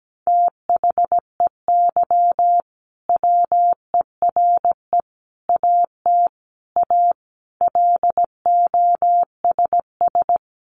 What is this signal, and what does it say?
Morse code, heard as a keyed tone, about 17 wpm, 700 Hz. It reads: THEY WERE AT A LOSS
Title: Sentences from Top 1400 Words 17wpm